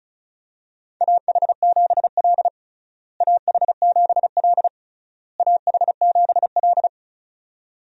Morse code, heard as a keyed tone, about 36 wpm, 700 Hz. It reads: AH7L AH7L AH7L
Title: USA Call Signs - 4 Characters - Repeated 3x - 35wpm